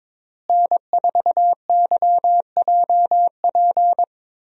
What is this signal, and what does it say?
Morse code, heard as a keyed tone, about 21 wpm, 700 Hz. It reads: N4YJP
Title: USA Call Signs - 5 Characters - Repeated 3x - 22wpm